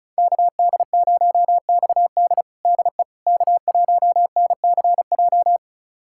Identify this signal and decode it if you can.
Morse code, heard as a keyed tone, about 36 wpm, 700 Hz. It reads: KD0XD DE K1NCJ
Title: USA Callsign DE Callsign - 5 Characters - 35wpm